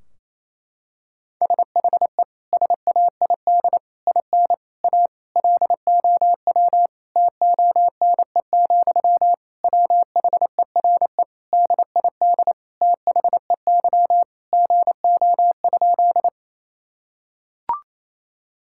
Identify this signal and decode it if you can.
Morse code, heard as a keyed tone, about 28 wpm, 700 Hz. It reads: SHE SAID IN A LOW TONE, WHERE DID THEY GO?
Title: Sentences from Top 1600 Words 28wpm